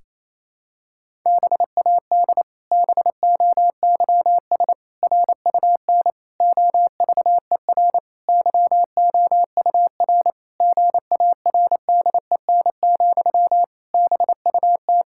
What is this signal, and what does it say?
Morse code, heard as a keyed tone, about 28 wpm, 700 Hz. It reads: BAD BOYS RUN OVER YOUR GARDEN, BUT
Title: Pangram Challenge v2 - 28wpm